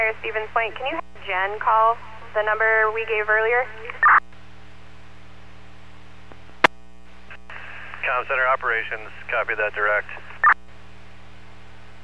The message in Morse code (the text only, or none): none